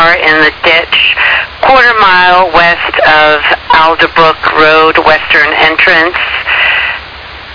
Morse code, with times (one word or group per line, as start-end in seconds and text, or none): none